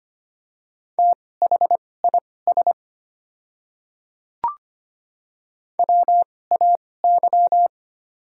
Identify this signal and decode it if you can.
Morse code, heard as a keyed tone, about 25 wpm, spaced as for 18 wpm, 700 Hz. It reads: THIS WAY
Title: 2x Character Spacing - Top 200 Words - 25wpm